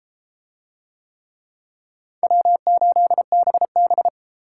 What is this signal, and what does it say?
Morse code, heard as a keyed tone, about 30 wpm, 700 Hz. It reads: W8BB